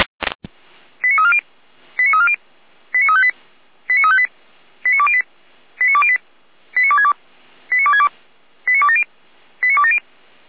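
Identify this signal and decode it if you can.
Five-tone selective call (ZVEI), heard as quick runs of five tones, tones of about 70 ms. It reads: 89340 89340 89358 89358 89208 89208 89262 89262 89270 89270
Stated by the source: ZVEI five-tone selective call received on an SDR.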